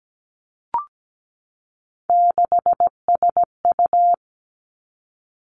Morse code, heard as a keyed tone, about 17 wpm, 700 Hz. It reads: 6SU
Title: Three Challenging Letter-Number Combinations 17wpm